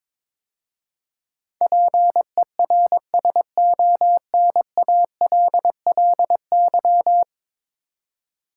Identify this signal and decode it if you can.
Morse code, heard as a keyed tone, about 22 wpm, 700 Hz. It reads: PERSONALLY
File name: practice / 22wpm / 10 Letter Words - First 3 Letters Given - 22wpm